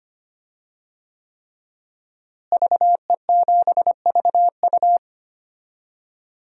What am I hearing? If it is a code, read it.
Morse code, heard as a keyed tone, about 25 wpm, 700 Hz. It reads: VE7VU
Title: DXCC-100 Call Signs - 5 Characters - Repeated 3x - 25wpm